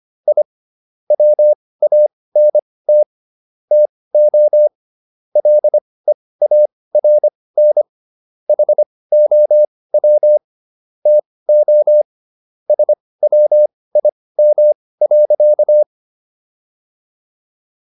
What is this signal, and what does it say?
Morse code, heard as a keyed tone, about 25 wpm, spaced as for 18 wpm, 600 Hz. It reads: I WANT TO LEARN HOW TO SWIM.